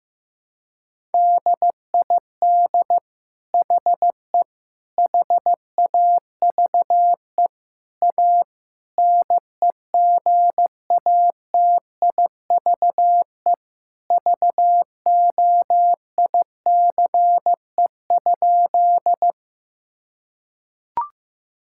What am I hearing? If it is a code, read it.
Morse code, heard as a keyed tone, about 15 wpm, 700 Hz. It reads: DID HE HAVE A NEGATIVE VOICE?
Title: Sentences from Top 1000 Words - Review Part 6 - 15wpm